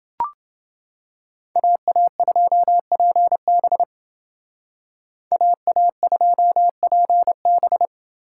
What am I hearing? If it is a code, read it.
Morse code, heard as a keyed tone, about 28 wpm, 700 Hz. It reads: AA2PB AA2PB